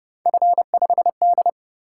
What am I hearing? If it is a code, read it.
Morse code, heard as a keyed tone, about 29 wpm, 700 Hz. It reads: F5D